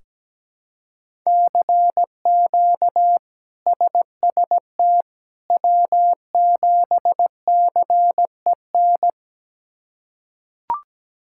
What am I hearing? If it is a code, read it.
Morse code, heard as a keyed tone, about 17 wpm, 700 Hz. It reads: CQ SST W7CEN
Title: K1USN SST QSO Practice - 17wpm